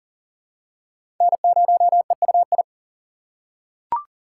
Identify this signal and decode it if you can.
Morse code, heard as a keyed tone, about 37 wpm, 700 Hz. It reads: N0EUI
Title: USA Call Signs - 5 Characters - 3x Character Spacing - 40wpm